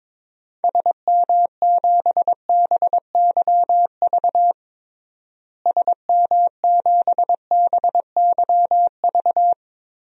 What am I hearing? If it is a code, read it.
Morse code, heard as a keyed tone, about 22 wpm, 700 Hz. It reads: SM7BYV SM7BYV
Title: DXCC-100 Call Signs - 6 Characters - Repeated 3x - 22wpm